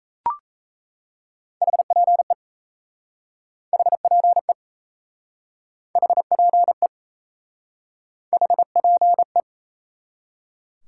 Morse code, heard as a keyed tone, about 37 wpm, 700 Hz. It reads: HPE HPE HPE HPE